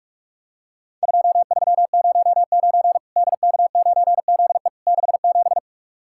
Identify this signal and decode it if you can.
Morse code, heard as a keyed tone, about 44 wpm, 700 Hz. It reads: 2309 DK9ZE 67
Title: Worked All Europe DX Contest - QTC Exchange - 45wpm